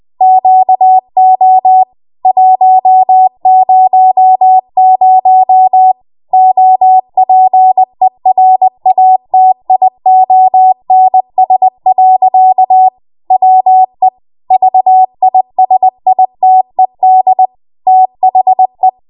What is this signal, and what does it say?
Morse code, heard as a keyed tone, about 20 wpm, 750 Hz. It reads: QO 100 OPERATIONS. WE VISITED THE
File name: Morse / 20WPM